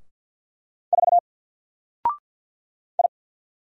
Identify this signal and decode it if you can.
Morse code, heard as a keyed tone, about 50 wpm, 700 Hz. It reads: 4 I
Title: Single Letter-Number 50wpm